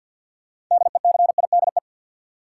Morse code, heard as a keyed tone, about 47 wpm, 700 Hz. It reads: DECIDE
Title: Sentences from Top 500 Words - New Word Drill - 50wpm